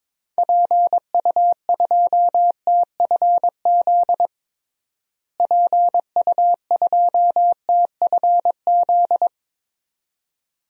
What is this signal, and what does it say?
Morse code, heard as a keyed tone, about 21 wpm, 700 Hz. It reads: PU2TFZ PU2TFZ
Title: DXCC-100 Call Signs - 6 Characters - Repeated 2x - 22wpm